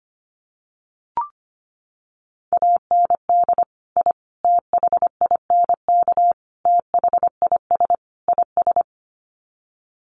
Morse code, heard as a keyed tone, about 25 wpm, 700 Hz. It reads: AND I THINK THIS IS
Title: Sets of 5 Words Encore 25wpm